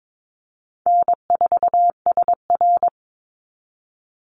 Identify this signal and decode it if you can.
Morse code, heard as a keyed tone, about 22 wpm, 690 Hz. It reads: N4SR
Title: Easy Call Signs Repeated 2x - 22wpm